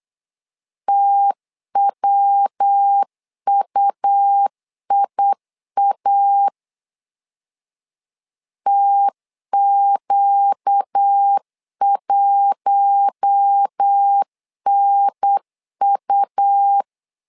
Morse code, heard as a keyed tone, about 8 wpm, 800 Hz. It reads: TWUIA TQ1NU
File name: LEZIONI / 4 Q1 8wpm 3035